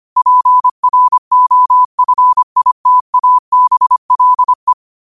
Morse code, heard as a keyed tone, about 25 wpm, 1000 Hz. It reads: PROFITABLE
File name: qotdfiles / 25wpm